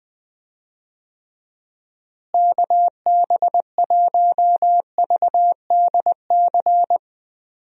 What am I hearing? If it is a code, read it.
Morse code, heard as a keyed tone, about 20 wpm, 700 Hz. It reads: KB1VDC
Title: USA Call Signs - 6 Characters - Repeated 3x - 20wpm